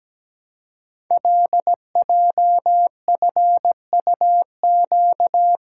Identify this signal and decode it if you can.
Morse code, heard as a keyed tone, about 17 wpm, 700 Hz. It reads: LJFUQ
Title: Code Groups - 5 Letters - 17wpm